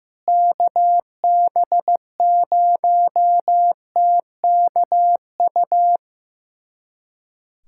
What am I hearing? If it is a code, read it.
Morse code, heard as a keyed tone, about 15 wpm, 700 Hz. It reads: KB0TKU
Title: USA Callsign DE Callsign - 6 Characters - 15wpm